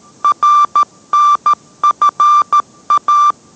Morse code, heard as a keyed tone, about 13 wpm, 1180 Hz. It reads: RNFA